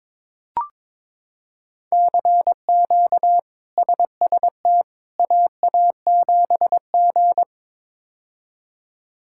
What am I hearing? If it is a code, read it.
Morse code, heard as a keyed tone, about 22 wpm, 700 Hz. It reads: CQ SST AA7G
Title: K1USN SST QSO Practice - 22wpm